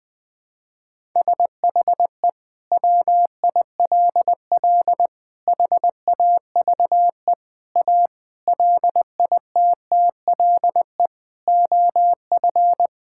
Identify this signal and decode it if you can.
Morse code, heard as a keyed tone, about 20 wpm, 700 Hz. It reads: SHE WILL HAVE A LITTLE OF
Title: Sentences from Top 1000 Words - Review Part 5 - 20wpm